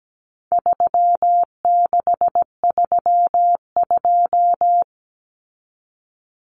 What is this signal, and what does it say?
Morse code, heard as a keyed tone, about 17 wpm, 700 Hz. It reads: 3632
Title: Common Cut Numbers - 4-Digit 17wpm